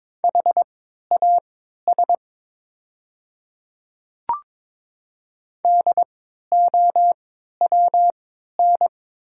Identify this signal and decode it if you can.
Morse code, heard as a keyed tone, about 22 wpm, spaced as for 12 wpm, 700 Hz. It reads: HAS DOWN
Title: 3x Character Spacing - Top 100 Words - 22wpm